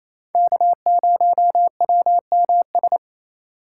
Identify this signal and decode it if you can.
Morse code, heard as a keyed tone, about 35 wpm, 700 Hz. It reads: K0WMS
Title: Speed Racing - USA 1x3 Call Signs 28wpm